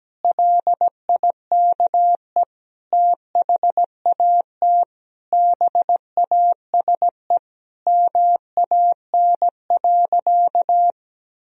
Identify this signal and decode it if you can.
Morse code, heard as a keyed tone, about 17 wpm, 700 Hz. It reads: LIKE THAT BASE MAN.
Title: Sentences from Top 1000 Words - Review Part 1 - 17wpm